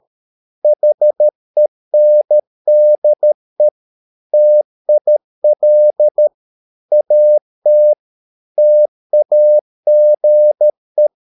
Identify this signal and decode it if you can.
Morse code, heard as a keyed tone, about 13 wpm, 600 Hz. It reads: HENDE TIL AT TAGE
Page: Ferie 13 wpm